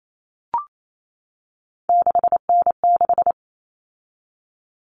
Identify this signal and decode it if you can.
Morse code, heard as a keyed tone, about 28 wpm, 700 Hz. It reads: 6N6
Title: All Cut Numbers - 3-Digit 28wpm